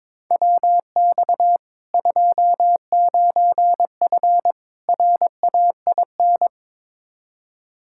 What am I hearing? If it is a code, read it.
Morse code, heard as a keyed tone, about 21 wpm, 700 Hz. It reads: WX 29F RAIN
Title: K1USN SST QSO Practice 22wpm